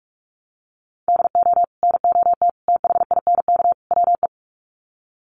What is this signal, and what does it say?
Morse code, heard as a keyed tone, about 44 wpm, 700 Hz. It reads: DO NOT THINK WE
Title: Sets of 4 Words 45wpm